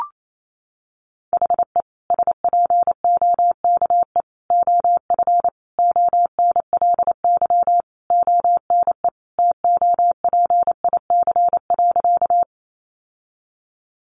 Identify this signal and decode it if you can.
Morse code, heard as a keyed tone, about 28 wpm, 700 Hz. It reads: HE SPOKE OF ONLY ONE TOPIC.
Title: Sentences from Top 1200 Words 28wpm